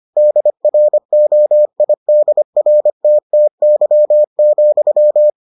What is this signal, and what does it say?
Morse code, heard as a keyed tone, about 25 wpm, 600 Hz. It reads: DROIDRTTY,